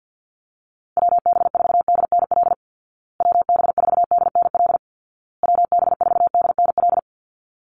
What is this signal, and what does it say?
Morse code, heard as a keyed tone, about 50 wpm, 700 Hz. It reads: WB4DNL WB4DNL WB4DNL
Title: Call Signs Repeated 2x - 50wpm